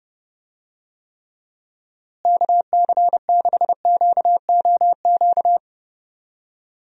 Morse code, heard as a keyed tone, about 31 wpm, 700 Hz. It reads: KC6QOQ
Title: USA Call Signs - 6 Characters - Repeated 3x - 30wpm